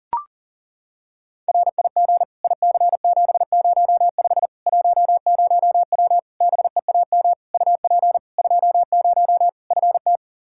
Morse code, heard as a keyed tone, about 40 wpm, 700 Hz. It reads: RIG IC705 10W BEAM UP 20 FT
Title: Practice QSO - Exchange 2 - Info - 40wpm